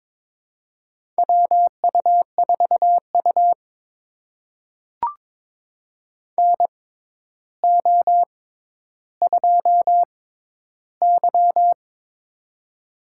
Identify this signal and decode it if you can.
Morse code, heard as a keyed tone, about 22 wpm, spaced as for 11 wpm, 700 Hz. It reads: WU4U NO2Y
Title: USA Call Signs - 4 Characters - 6x Character Spacing - 22wpm